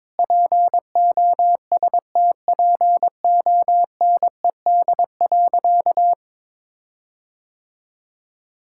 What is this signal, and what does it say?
Morse code, heard as a keyed tone, about 22 wpm, 700 Hz. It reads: POSTPONED.
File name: practice / 22wpm / Sentences with 8-Word Phrases - 22wpm